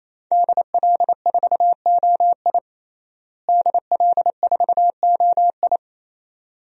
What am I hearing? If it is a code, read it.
Morse code, heard as a keyed tone, about 27 wpm, 700 Hz. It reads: DL4OI DL4OI
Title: USA Call Signs - 5 Characters - Repeated 2x - 28wpm